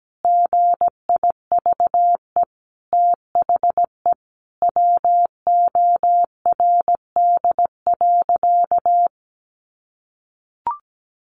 Morse code, heard as a keyed tone, about 17 wpm, 700 Hz. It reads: GIVE THE WORD.
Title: A Sound Of Thunder 17wpm